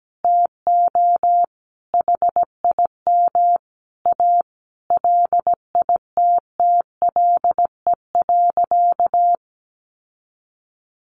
Morse code, heard as a keyed tone, about 17 wpm, 700 Hz. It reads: TO HIM A LITTLE.
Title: Sentences from Top 1500 Words 17wpm